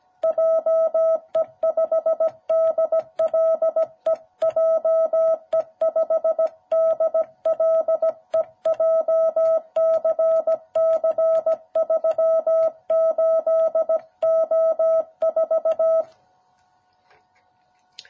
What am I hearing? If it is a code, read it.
Morse code, heard as a keyed tone, about 17 wpm, 660 Hz. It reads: JE5DLE JE5DLE JCC38O4
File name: jh8cbh / 2021cw